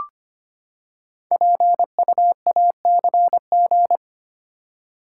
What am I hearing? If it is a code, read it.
Morse code, heard as a keyed tone, about 25 wpm, 700 Hz. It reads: PUACG
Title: Code Groups - 5 Letters - 25wpm